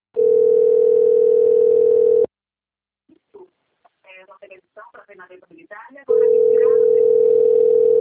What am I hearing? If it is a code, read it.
Ringback tone (North America), 440 + 480 Hz, with speech.
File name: rg-1600-unknown- - - .17187